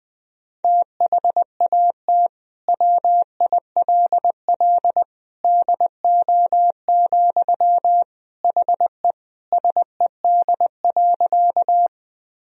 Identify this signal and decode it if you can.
Morse code, heard as a keyed tone, about 21 wpm, 700 Hz. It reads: THAT WILL DO, HE SED.
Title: Sentences from Top 100 Words and Abbreviations 20wpm